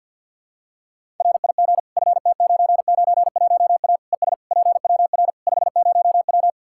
Morse code, heard as a keyed tone, about 50 wpm, 700 Hz. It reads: RIG FT991A ES PWR 50W
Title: K1USN SST QSO Practice 50wpm